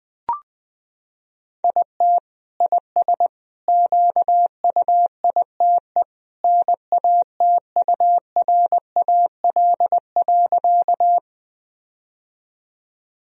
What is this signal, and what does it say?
Morse code, heard as a keyed tone, about 21 wpm, 700 Hz. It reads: IT IS QUITE NATURAL.
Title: Sentences from Top 1000 Words - Review Part 4 - 20wpm